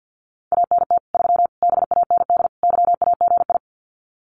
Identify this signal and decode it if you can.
Morse code, heard as a keyed tone, about 50 wpm, 700 Hz. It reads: ANT 3 BAND YAGI
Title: Common QSO Phrases 50wpm